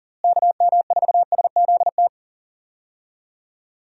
Morse code, heard as a keyed tone, about 40 wpm, 700 Hz. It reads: KM4SZT
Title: USA Call Signs - 6 Characters - Repeated 3x - 40wpm